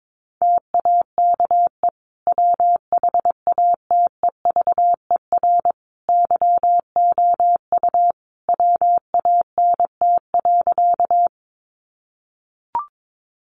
Morse code, heard as a keyed tone, about 22 wpm, 700 Hz. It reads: TAKE WHATEVER YOU WANT.
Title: Sentences from Top 1100 Words 22wpm